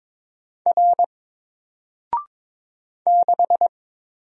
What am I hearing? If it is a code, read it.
Morse code, heard as a keyed tone, about 22 wpm, 700 Hz. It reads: R 6
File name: Single-Letter-Number-22wpm